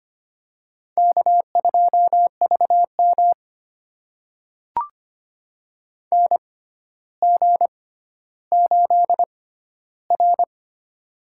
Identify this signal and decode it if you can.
Morse code, heard as a keyed tone, about 25 wpm, spaced as for 13 wpm, 700 Hz. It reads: K2VM NG8R
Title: USA Call Signs - 4 Characters - 6x Character Spacing - 25wpm